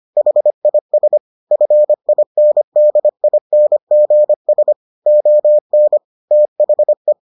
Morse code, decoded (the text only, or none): HIS FINDINGS ON THE